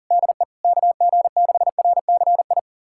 Morse code, heard as a keyed tone, about 40 wpm, 700 Hz. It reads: DE KG6RCI
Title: USA Callsign DE Callsign - 6 Characters - 40wpm